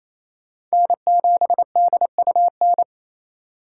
Morse code, heard as a keyed tone, about 35 wpm, 700 Hz. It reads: N7DUN